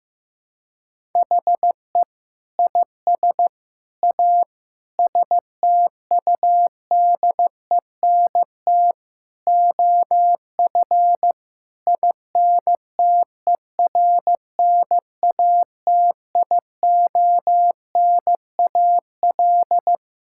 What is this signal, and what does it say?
Morse code, heard as a keyed tone, about 15 wpm, 700 Hz. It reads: HE IS A STUDENT OF INTERNATIONAL
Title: Sentences from Top 1000 Words - Review Part 6 - 15wpm